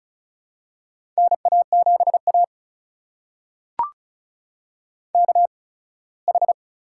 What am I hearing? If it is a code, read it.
Morse code, heard as a keyed tone, about 34 wpm, spaced as for 21 wpm, 700 Hz. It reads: NA7A KH
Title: USA Call Signs - 4 Characters - 8x Character Spacing - 35wpm